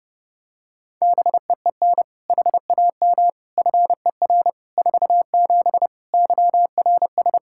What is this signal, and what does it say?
Morse code, heard as a keyed tone, about 32 wpm, 700 Hz. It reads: BEEN HAM FER 47 YRS
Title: Practice QSO - Exchange 3 - More Info - 30wpm